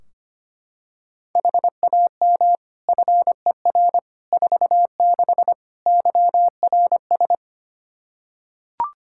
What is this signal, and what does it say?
Morse code, heard as a keyed tone, about 25 wpm, 700 Hz. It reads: HAM FER 46 YRS
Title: Practice QSO - Exchange 3 - More Info - 25wpm